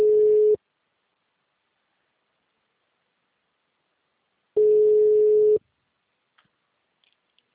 Ringback tone (Europe, 425 Hz).